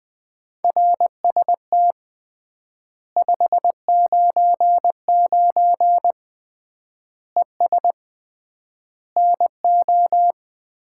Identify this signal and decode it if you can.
Morse code, heard as a keyed tone, about 20 wpm, 700 Hz. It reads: RST 599 ES NO
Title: Common QSO Phrases - 3x Word-Spacing - 20wpm